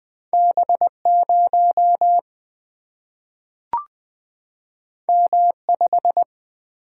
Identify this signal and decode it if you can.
Morse code, heard as a keyed tone, about 20 wpm, 700 Hz. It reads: B 0 M 5
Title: Single-Letter Single-Number 20wpm